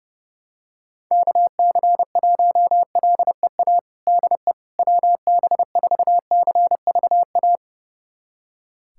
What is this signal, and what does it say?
Morse code, heard as a keyed tone, about 32 wpm, 700 Hz. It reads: KC1LEA DE WB4CVA
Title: USA Callsign DE Callsign 30wpm